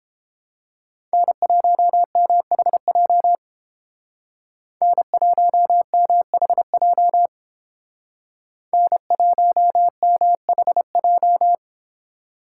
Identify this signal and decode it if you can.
Morse code, heard as a keyed tone, about 28 wpm, 700 Hz. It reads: N1MHJ N1MHJ N1MHJ